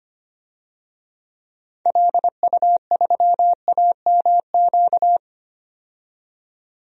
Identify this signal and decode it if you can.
Morse code, heard as a keyed tone, about 25 wpm, 700 Hz. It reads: LU3AMQ
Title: DXCC-100 Call Signs - 6 Characters - 25wpm